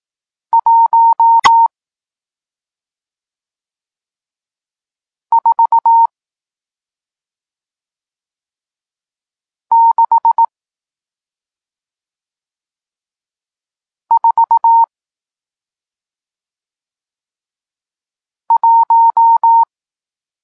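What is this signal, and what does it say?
Morse code, heard as a keyed tone, about 18 wpm, spaced as for 2 wpm, 920 Hz. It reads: 14641
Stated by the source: Morse code lesson teaching digits 1, 4 and 6, from SP9MOA, Niepołomicki Klub Krótkofalowców (Poland) amateur radio club.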